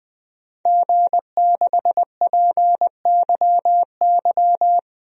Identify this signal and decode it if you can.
Morse code, heard as a keyed tone, about 20 wpm, 700 Hz. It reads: G6PYY
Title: DXCC-100 Call Signs - 5 Characters - Repeated 3x - 20wpm